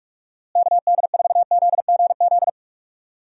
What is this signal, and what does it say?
Morse code, heard as a keyed tone, about 42 wpm, 700 Hz. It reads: KD4ZGZ
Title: USA Call Signs - 6 Characters - Repeated 3x - 45wpm